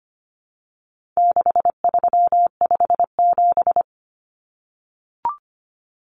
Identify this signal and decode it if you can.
Morse code, heard as a keyed tone, about 25 wpm, 700 Hz. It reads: 6357
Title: Common Cut Numbers - 4-Digit 25wpm